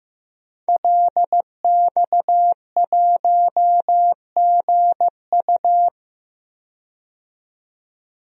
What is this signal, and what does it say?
Morse code, heard as a keyed tone, about 15 wpm, 700 Hz. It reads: LX1GU
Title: DXCC-100 Call Signs - 5 Characters - 15wpm